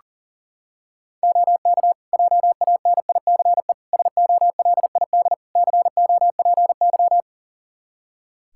Morse code, heard as a keyed tone, about 40 wpm, 700 Hz. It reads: OK JANICE SOLID COPY
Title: Practice QSO - Exchange 3 - More Info - 40wpm